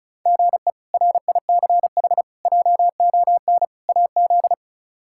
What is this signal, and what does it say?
Morse code, heard as a keyed tone, about 36 wpm, 700 Hz. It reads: GE RICH JON AZ
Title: K1USN SST QSO Practice - 35wpm